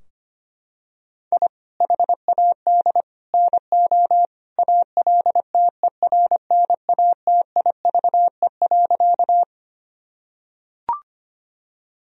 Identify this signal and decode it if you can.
Morse code, heard as a keyed tone, about 25 wpm, 700 Hz. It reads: I HAD NO ALTERNATIVE.
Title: Sentences from Top 1000 Words - Review Part 4 - 25wpm